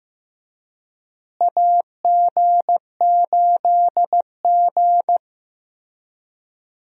Morse code, heard as a keyed tone, about 15 wpm, 700 Hz. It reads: AG8G
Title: USA Call Signs - 4 Characters - Repeated 3x - 15wpm